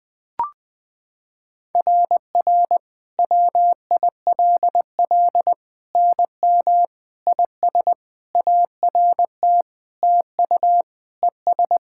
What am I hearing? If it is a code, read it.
Morse code, heard as a keyed tone, about 21 wpm, 700 Hz. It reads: RR WILL NM IS ART TU ES
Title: SOTA QSO Practice 20wpm